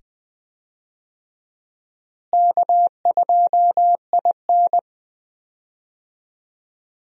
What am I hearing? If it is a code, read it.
Morse code, heard as a keyed tone, about 20 wpm, 700 Hz. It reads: K2IN
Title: USA Call Signs - 4 Characters - Repeated 2x - 20wpm